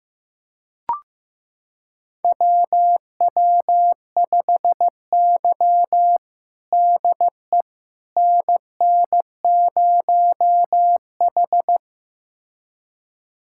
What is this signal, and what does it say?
Morse code, heard as a keyed tone, about 15 wpm, 700 Hz. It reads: WW5Y DE NN0H
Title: USA Callsign DE Callsign - 4 Characters - 15wpm